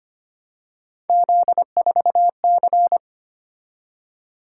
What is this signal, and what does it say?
Morse code, heard as a keyed tone, about 25 wpm, 700 Hz. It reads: Z4C
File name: Three Letter-Number Combinations - 25wpm